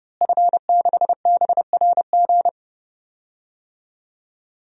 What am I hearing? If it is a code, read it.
Morse code, heard as a keyed tone, about 31 wpm, 700 Hz. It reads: F6BRG
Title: DXCC-100 Call Signs - 5 Characters - Repeated 2x - 30wpm